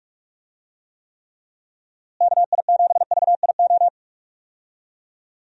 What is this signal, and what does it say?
Morse code, heard as a keyed tone, about 42 wpm, 700 Hz. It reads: KI7VIO
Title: USA Call Signs - 6 Characters - Repeated 3x - 45wpm